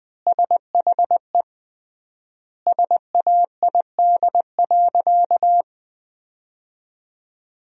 Morse code, heard as a keyed tone, about 20 wpm, 700 Hz. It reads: SHE SAID.
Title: 3x Word-Spacing - Sentences from Top 400 Words - 20wpm